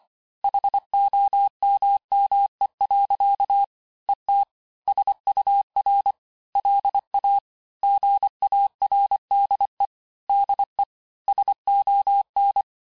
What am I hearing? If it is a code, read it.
Morse code, heard as a keyed tone, about 25 wpm, 800 Hz. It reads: HOMME. ET SUR LA GARDE DE SON
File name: cours-868-teste-clair-vitesse-23mpm